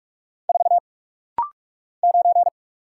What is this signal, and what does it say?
Morse code, heard as a keyed tone, about 43 wpm, 690 Hz. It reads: 4 9
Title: Single Letter-Number - Rapid-Fire 45wpm